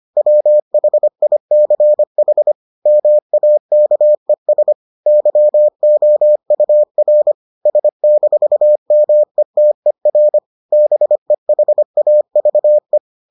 Morse code, heard as a keyed tone, about 25 wpm, 600 Hz. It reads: WHICH MAKES YOUR S-METER BEHAVE